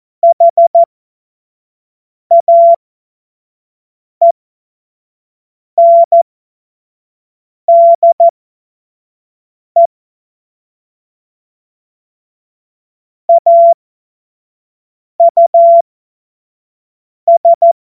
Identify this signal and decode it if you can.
Morse code, heard as a keyed tone, about 14 wpm, spaced as for 5 wpm, 680 Hz. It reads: HAENDE AUS
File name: Morsekurs / Witze / Witz 24.Oktober 2023 25 F70